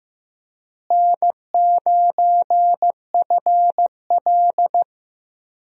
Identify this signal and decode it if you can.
Morse code, heard as a keyed tone, about 15 wpm, 700 Hz. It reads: N9FL
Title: USA Call Signs - 4 Characters - Repeated 3x - 15wpm